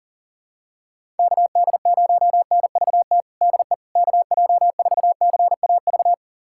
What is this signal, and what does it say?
Morse code, heard as a keyed tone, about 40 wpm, 700 Hz. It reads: KD0NVT DE KJ4CAV
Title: USA Callsign DE Callsign 40wpm